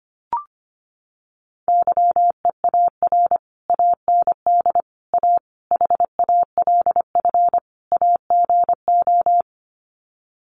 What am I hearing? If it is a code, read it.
Morse code, heard as a keyed tone, about 25 wpm, 700 Hz. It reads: YEAR AND A HALF AGO
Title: Sets of 5 Words Encore 25wpm